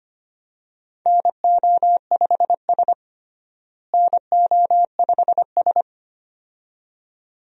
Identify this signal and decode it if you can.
Morse code, heard as a keyed tone, about 25 wpm, 700 Hz. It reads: NO5S NO5S
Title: USA Call Signs - 4 Characters - Repeated 2x - 25wpm